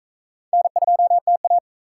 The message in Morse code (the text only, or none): N2TA